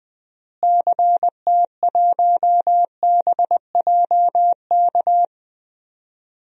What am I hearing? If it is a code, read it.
Morse code, heard as a keyed tone, about 20 wpm, 700 Hz. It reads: CT1BJK
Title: DXCC-100 Call Signs - 6 Characters - Repeated 2x - 20wpm